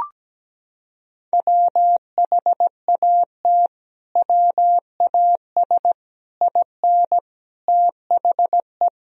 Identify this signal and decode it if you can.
Morse code, heard as a keyed tone, about 17 wpm, 700 Hz. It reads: WHAT WAS IN THE
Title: Sentences from Top 1000 Words - Review Part 2 - 17wpm